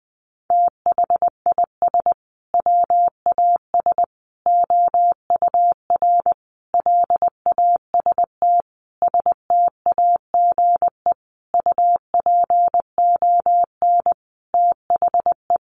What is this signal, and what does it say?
Morse code, heard as a keyed tone, about 20 wpm, 700 Hz. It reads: THIS WAS OUR LAST STAGE UPON THE
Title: Sentences from Top 1200 Words 20wpm